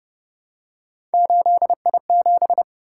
Morse code, heard as a keyed tone, about 31 wpm, 700 Hz. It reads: 8I7
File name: Speed-Racing - Three Letter-Number Combinations - 25wpm